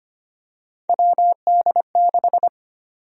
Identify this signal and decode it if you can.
Morse code, heard as a keyed tone, about 25 wpm, 700 Hz. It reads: WD6
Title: USA Call Sign Prefixes - Repeated 2x - 25wpm